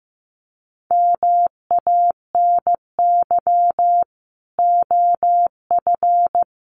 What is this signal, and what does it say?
Morse code, heard as a keyed tone, about 15 wpm, 700 Hz. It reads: MANY OF
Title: Sets of 3 Words 15wpm